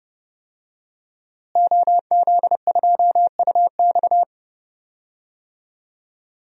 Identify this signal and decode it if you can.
Morse code, heard as a keyed tone, about 30 wpm, 700 Hz. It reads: OZ2UX
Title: DXCC-100 Call Signs - 5 Characters - Repeated 3x - 30wpm